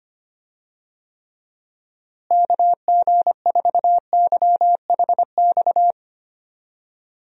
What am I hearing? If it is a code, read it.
Morse code, heard as a keyed tone, about 25 wpm, 700 Hz. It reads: KG4YHX
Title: USA Call Signs - 6 Characters - Repeated 3x - 25wpm